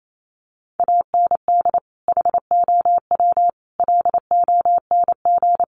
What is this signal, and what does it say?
Morse code, heard as a keyed tone, about 28 wpm, 700 Hz. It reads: AND HOW LONG
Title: Sentences from Top 1200 Words 28wpm